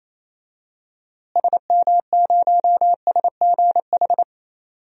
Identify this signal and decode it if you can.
Morse code, heard as a keyed tone, about 28 wpm, 700 Hz. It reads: SM0SGH
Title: DXCC-100 Call Signs - 6 Characters - Repeated 2x - 28wpm